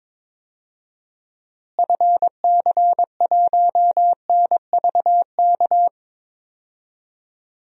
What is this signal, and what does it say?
Morse code, heard as a keyed tone, about 22 wpm, 700 Hz. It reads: FC1NVK
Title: DXCC-100 Call Signs - 6 Characters - Repeated 3x - 22wpm